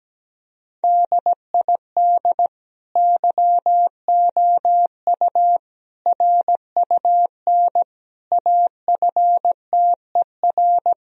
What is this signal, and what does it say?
Morse code, heard as a keyed tone, about 17 wpm, 700 Hz. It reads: DID YOU RUN AFTER
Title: Sentences from Top 1000 Words - Review Part 1 - 17wpm